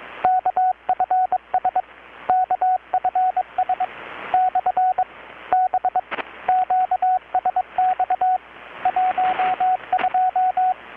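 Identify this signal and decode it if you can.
Morse code, heard as a keyed tone, about 22 wpm, 740 Hz, with radio noise. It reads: KFS KFS / B QSX 12
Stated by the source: Morse from KPH's Night of Nights XX maritime radiotelegraphy event, received in Chiapas, Mexico about 3,800 km away on a Kenwood R-600 with a grounded 10 m long-wire antenna.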